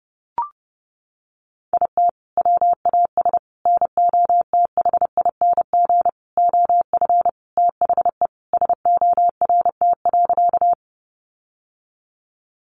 Morse code, heard as a keyed tone, about 30 wpm, 700 Hz. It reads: IT WAS NOTHING OF THE SORT.
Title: Sentences from Top 1300 Words 30wpm